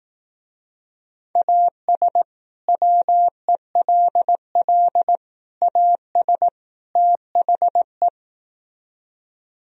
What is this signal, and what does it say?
Morse code, heard as a keyed tone, about 18 wpm, 700 Hz. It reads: AS WELL AS THE
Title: Speed-Racing - Sets of 4 Words 15wpm